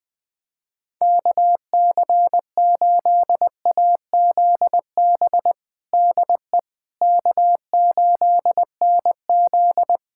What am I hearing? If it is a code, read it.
Morse code, heard as a keyed tone, about 20 wpm, 700 Hz. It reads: KC8AZB DE K8NZ
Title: Practice QSO - Exchange 4 - Ending - 20wpm